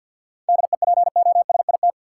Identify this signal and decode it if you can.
Morse code, heard as a keyed tone, about 46 wpm, 700 Hz. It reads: DEPOSIT
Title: Sentences from Top 1300 Words - New Word Drill - 50wpm